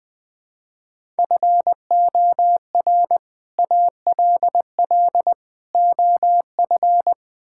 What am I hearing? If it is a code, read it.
Morse code, heard as a keyed tone, about 20 wpm, 700 Hz. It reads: FOR ALL OF
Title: Build A Phrase - 4-Word Phrases - 20wpm